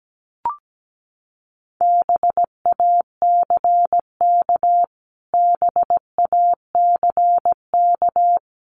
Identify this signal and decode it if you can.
Morse code, heard as a keyed tone, about 17 wpm, 700 Hz. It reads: BACK BACK
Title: Top 500 Words Repeated 17wpm